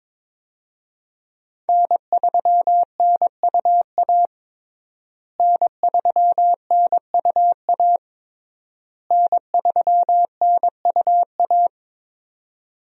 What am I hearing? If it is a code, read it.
Morse code, heard as a keyed tone, about 22 wpm, 700 Hz. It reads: N3NUA N3NUA N3NUA
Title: USA Call Signs - 5 Characters - Repeated 3x - 22wpm